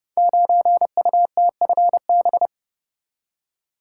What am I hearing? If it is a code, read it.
Morse code, heard as a keyed tone, about 30 wpm, 700 Hz. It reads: 9UTFB
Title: Code Groups - 5 Letter:Numbers - 30wpm